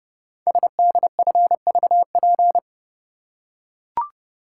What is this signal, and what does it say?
Morse code, heard as a keyed tone, about 30 wpm, 700 Hz. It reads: SDFVP
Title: Code Groups - 5 Letter:Numbers - 30wpm